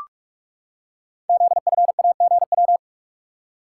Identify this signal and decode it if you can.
Morse code, heard as a keyed tone, about 44 wpm, 700 Hz. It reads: ZFAGW
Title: Code Groups - 5 Letters - 45wpm